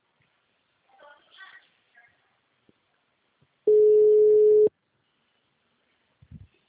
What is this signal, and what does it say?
Ringback tone (Europe, 425 Hz).
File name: out- -2011- - - .73577